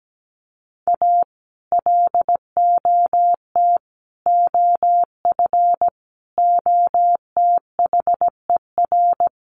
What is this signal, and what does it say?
Morse code, heard as a keyed tone, about 17 wpm, 700 Hz. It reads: A LOT OF OTHER
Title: Sets of 5 Words Encore2 17wpm